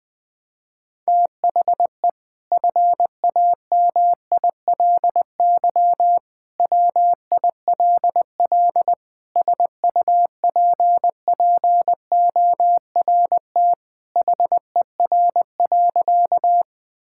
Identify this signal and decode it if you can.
Morse code, heard as a keyed tone, about 21 wpm, 700 Hz. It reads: THE FAMILY WILL SUPPORT HER.
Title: Sentences from Top 1000 Words - Review Part 5 - 20wpm